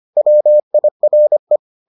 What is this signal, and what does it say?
Morse code, heard as a keyed tone, about 25 wpm, 600 Hz. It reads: WIRE